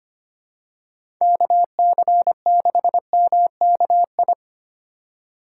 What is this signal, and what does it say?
Morse code, heard as a keyed tone, about 25 wpm, 700 Hz. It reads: KC6MKI
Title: USA Call Signs - 6 Characters - 3x Character Spacing - 25wpm